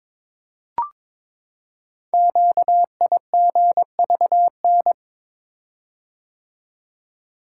Morse code, heard as a keyed tone, about 22 wpm, 700 Hz. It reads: QIGVN
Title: Code Groups - 5 Letters - 22wpm